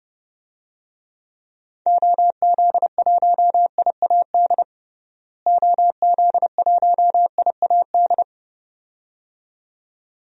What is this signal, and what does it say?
Morse code, heard as a keyed tone, about 30 wpm, 700 Hz. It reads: OZ1IAD OZ1IAD
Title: DXCC-100 Call Signs - 6 Characters - Repeated 2x - 30wpm